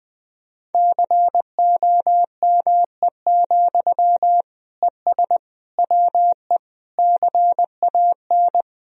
Morse code, heard as a keyed tone, about 21 wpm, 700 Hz. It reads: COME, ES WE CAN
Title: Sentences from Top 500 Words and Abbreviations 20wpm